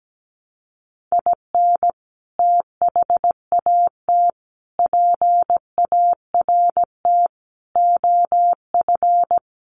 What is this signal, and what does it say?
Morse code, heard as a keyed tone, about 17 wpm, 700 Hz. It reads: IN THAT PART OF
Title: Sets of 5 Words Encore2 17wpm